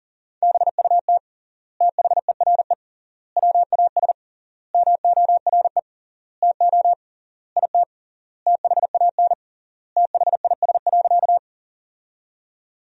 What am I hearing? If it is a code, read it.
Morse code, heard as a keyed tone, about 40 wpm, 700 Hz. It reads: BUT THERE WAS MORE TO IT THAN THIS.
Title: 3x Word-Spacing - Sentences from Top 100 Words - 40wpm